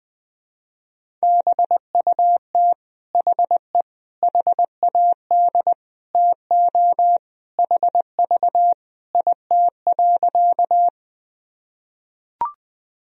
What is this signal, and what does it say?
Morse code, heard as a keyed tone, about 21 wpm, 700 Hz. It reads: BUT HE HAD TO HV IT.
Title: Sentences from Top 100 Words and Abbreviations 20wpm